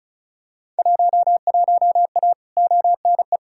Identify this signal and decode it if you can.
Morse code, heard as a keyed tone, about 36 wpm, 700 Hz. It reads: 11A ONE
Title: Field Day Exchanges Encore - 35wpm